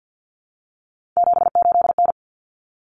Morse code, heard as a keyed tone, about 50 wpm, 700 Hz. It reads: 78N
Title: Common Cut Numbers - 3-Digit 50wpm